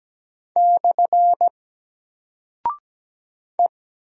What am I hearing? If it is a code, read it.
Morse code, heard as a keyed tone, about 17 wpm, 700 Hz. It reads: / E
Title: Single Character (Letter:Number:Punctuation:Prosign) Alt 17wpm